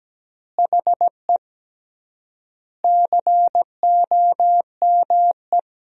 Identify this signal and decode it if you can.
Morse code, heard as a keyed tone, about 17 wpm, 700 Hz. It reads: HE COME
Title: 3x Word-Spacing - Sentences from Top 200 Words - 17wpm